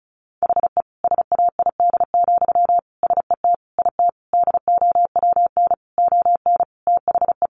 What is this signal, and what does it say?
Morse code, heard as a keyed tone, about 34 wpm, 700 Hz. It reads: HE SAID, SET IT DOWN ON THE
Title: Sentences from Top 1400 Words 35wpm